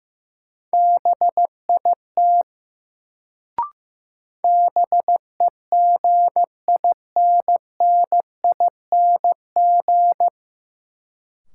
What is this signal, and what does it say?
Morse code, heard as a keyed tone, about 15 wpm, 700 Hz. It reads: BIT BEGINNING
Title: Sets of 3 Words - New Word Drill - 15wpm